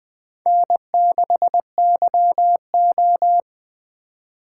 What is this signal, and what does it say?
Morse code, heard as a keyed tone, about 20 wpm, 700 Hz. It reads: N6YO
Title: USA Call Signs - 4 Characters - 4x Character Spacing - 20wpm